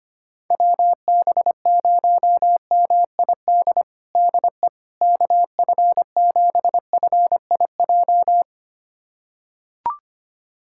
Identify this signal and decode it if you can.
Morse code, heard as a keyed tone, about 25 wpm, 700 Hz. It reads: WB0MID DE KF7FIJ
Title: USA Callsign DE Callsign - 6 Characters - 25wpm